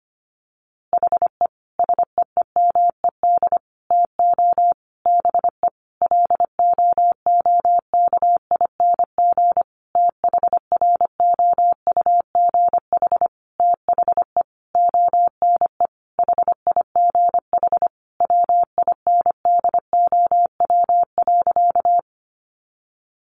Morse code, heard as a keyed tone, about 25 wpm, 700 Hz. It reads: HE SEEMED TO BE LOOKING THROUGH THE ONE HIGH WINDOW.
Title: A Sound Of Thunder 25wpm